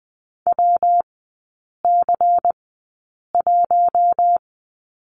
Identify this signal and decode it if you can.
Morse code, heard as a keyed tone, about 20 wpm, spaced as for 8 wpm, 700 Hz. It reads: WC1
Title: 114 Lesson 19e - Letter w - Three Characters 20wpm